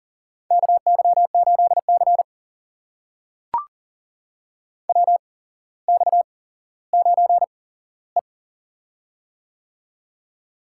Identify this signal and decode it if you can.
Morse code, heard as a keyed tone, about 40 wpm, spaced as for 17 wpm, 700 Hz. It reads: KY8C WX9E
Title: USA Call Signs - 4 Characters - 8x Character Spacing - 40wpm